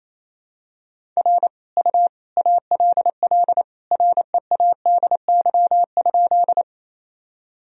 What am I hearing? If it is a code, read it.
Morse code, heard as a keyed tone, about 28 wpm, 700 Hz. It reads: R U ALL READY?
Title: Sentences from Top 400 Words and Abbreviations 28wpm